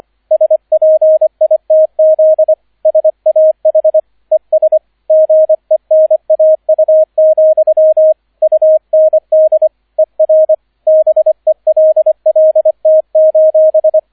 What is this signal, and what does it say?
Morse code, heard as a keyed tone, about 25 wpm, 630 Hz, with radio noise. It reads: SPITZ SAH ES GENAU, UND ER BELLT: